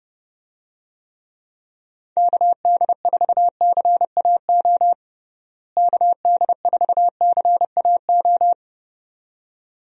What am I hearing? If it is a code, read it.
Morse code, heard as a keyed tone, about 30 wpm, 700 Hz. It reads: KD4CAO KD4CAO
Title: USA Call Signs - 6 Characters - Repeated 2x - 30wpm